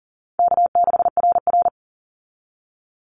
Morse code, heard as a keyed tone, about 40 wpm, 700 Hz. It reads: K6RR
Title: Call Signs Repeated 3x - 40wpm